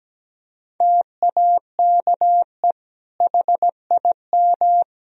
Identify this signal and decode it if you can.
Morse code, heard as a keyed tone, about 17 wpm, 700 Hz. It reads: TAKE HIM
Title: Sentences from Top 1000 Words - Review Part 4 - 17wpm